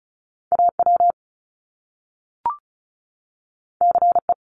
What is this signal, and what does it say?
Morse code, heard as a keyed tone, about 36 wpm, 700 Hz. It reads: AW CE
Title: Top Two Letter Combinations in Words 35wpm